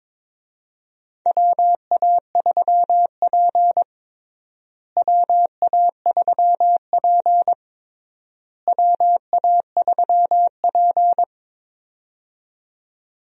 Morse code, heard as a keyed tone, about 22 wpm, 700 Hz. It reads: WA3P WA3P WA3P
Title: USA Call Signs - 4 Characters - Repeated 3x - 22wpm